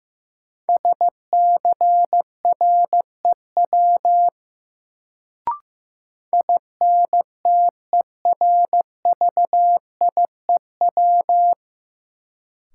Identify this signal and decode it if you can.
Morse code, heard as a keyed tone, about 15 wpm, 700 Hz. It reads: SCREW INTERVIEW
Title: Sentences from Top 1600 Words - New Word Drill - 15wpm